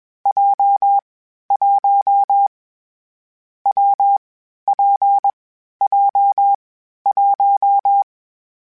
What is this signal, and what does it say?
Morse code, heard as a keyed tone, about 21 wpm, 800 Hz. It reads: J 1 W P J 1
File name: Variable Speed / Variable Speed-mp3s / WPJ WPJ1 sp @20